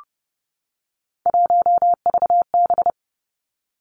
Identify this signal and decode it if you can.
Morse code, heard as a keyed tone, about 30 wpm, 700 Hz. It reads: 1VB
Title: Three Challenging Letter-Number Combinations 30wpm